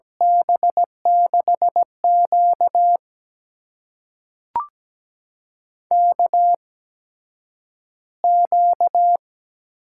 Morse code, heard as a keyed tone, about 17 wpm, spaced as for 9 wpm, 700 Hz. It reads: B6Q KQ
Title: USA Call Signs - 4 Characters - 8x Character Spacing - 17wpm